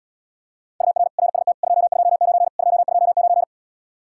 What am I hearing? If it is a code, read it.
Morse code, heard as a keyed tone, about 80 wpm, 700 Hz. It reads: UR RST 229 229